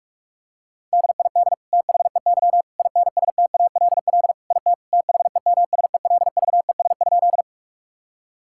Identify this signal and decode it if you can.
Morse code, heard as a keyed tone, about 46 wpm, 700 Hz. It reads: DID THEY INSTALL IT THEMSELVES?
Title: Sentences from Top 1600 Words 45wpm